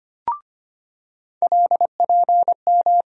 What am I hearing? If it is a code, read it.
Morse code, heard as a keyed tone, about 25 wpm, 700 Hz. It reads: LPM
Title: USA Call Sign Postfixes - 3 Letters 25wpm